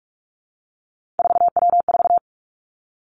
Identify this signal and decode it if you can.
Morse code, heard as a keyed tone, about 46 wpm, 700 Hz. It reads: 4W4
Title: Three Challenging Letter-Number Combinations 45wpm